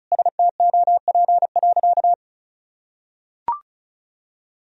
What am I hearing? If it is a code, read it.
Morse code, heard as a keyed tone, about 36 wpm, 700 Hz. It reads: STOP.
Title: Sentences from Top 1000 Words - Review Part 4 - 35wpm